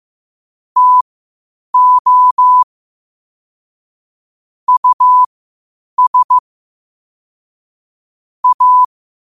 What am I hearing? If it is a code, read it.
Morse code, heard as a keyed tone, about 15 wpm, spaced as for 8 wpm, 1000 Hz. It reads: TO US A
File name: qotdfiles / 8wpm